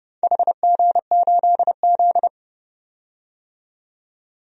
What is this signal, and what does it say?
Morse code, heard as a keyed tone, about 30 wpm, 700 Hz. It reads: HG8Z
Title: DXCC-100 Call Signs - 4 Characters - 30wpm